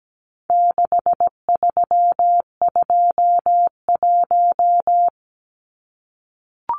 Morse code, heard as a keyed tone, about 17 wpm, 700 Hz. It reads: 6321
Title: Numbers - 4-Digit 17wpm